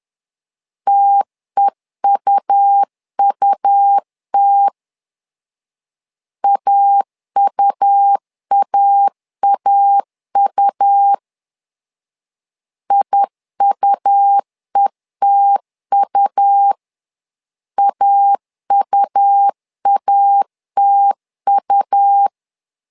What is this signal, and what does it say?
Morse code, heard as a keyed tone, about 10 wpm, 800 Hz. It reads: TEUUT AUAAU IUETU AUATU
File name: LEZIONI / 2 UHDN / 2 U 10wpm 1622